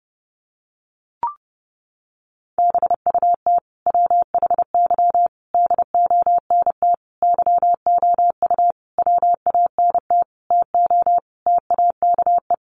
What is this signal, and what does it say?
Morse code, heard as a keyed tone, about 30 wpm, 700 Hz. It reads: BUT WHY DONT YOU WANT TO TAKE
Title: Sentences from Top 300 Words 30wpm